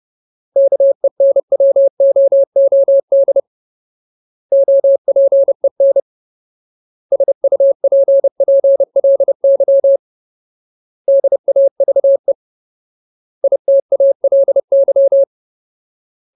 Morse code, heard as a keyed tone, about 30 wpm, 550 Hz. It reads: KENWOOD OPEN SUPPLY DAVE ITALY
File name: Ultimate Word List / 30WPM Ultimate